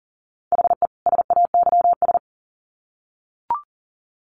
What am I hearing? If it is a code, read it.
Morse code, heard as a keyed tone, about 40 wpm, 700 Hz. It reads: HE SAYS
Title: Sets of 2 Words 40wpm